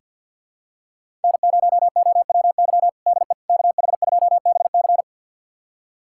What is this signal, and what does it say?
Morse code, heard as a keyed tone, about 47 wpm, 700 Hz. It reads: N0OWY DE KH2BC
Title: USA Callsign DE Callsign - 5 Characters - 50wpm